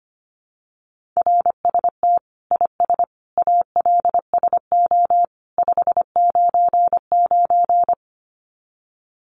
Morse code, heard as a keyed tone, about 25 wpm, 700 Hz. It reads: RST IS ALSO 599
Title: Common QSO Phrases 25wpm